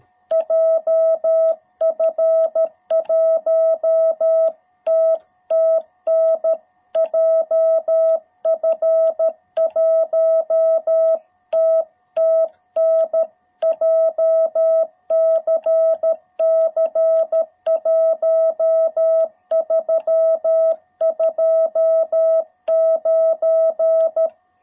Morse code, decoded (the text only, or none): JF1TTN JF1TTN JCC 1329